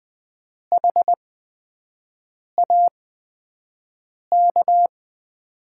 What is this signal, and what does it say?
Morse code, heard as a keyed tone, about 20 wpm, spaced as for 5 wpm, 700 Hz. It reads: HAK
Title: USA Call Signs - 6 Characters - 8x Character Spacing - 20wpm